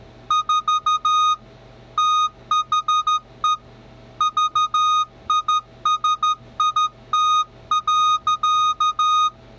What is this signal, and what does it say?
Morse code, heard as a keyed tone, about 13 wpm, 1240 Hz. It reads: 4 THE VISIT.